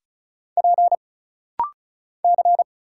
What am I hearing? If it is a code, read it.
Morse code, heard as a keyed tone, about 36 wpm, 700 Hz. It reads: P C
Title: Single Letter-Number Alt - Rapid-Fire 35wpm